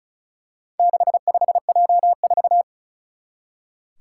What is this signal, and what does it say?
Morse code, heard as a keyed tone, about 36 wpm, 700 Hz. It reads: 65J4
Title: Four Challenging Letter-Number Combinations 35wpm